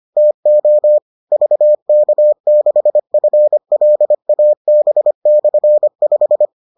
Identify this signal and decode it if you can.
Morse code, heard as a keyed tone, about 25 wpm, 600 Hz. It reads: TO VK6FLAB/5